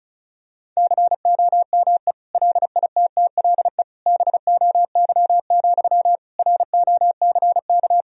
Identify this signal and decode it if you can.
Morse code, heard as a keyed tone, about 34 wpm, 700 Hz. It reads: COME LITTLE BOY, ROCK
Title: Sentences from Top 1000 Words - Review Part 2 - 35wpm